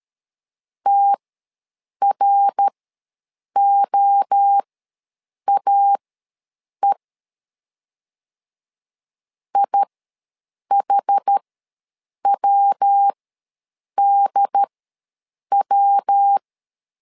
Morse code, heard as a keyed tone, about 13 wpm, spaced as for 7 wpm, 800 Hz. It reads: TROAE IHWDW
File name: LEZIONI / 3 MWRO / 3 MWRO ws 12wpm 1914